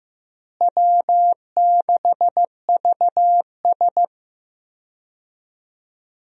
Morse code, heard as a keyed tone, about 15 wpm, 700 Hz. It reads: W6VS
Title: USA Call Signs - 4 Characters - Repeated 3x - 15wpm